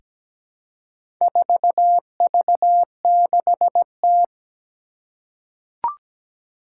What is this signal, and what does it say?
Morse code, heard as a keyed tone, about 17 wpm, 700 Hz. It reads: 4V6T
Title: Four Challenging Letter-Number Combinations 17wpm